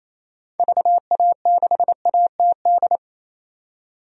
Morse code, heard as a keyed tone, about 27 wpm, 700 Hz. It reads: VA6ATD
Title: DXCC-100 Call Signs - 6 Characters - 28wpm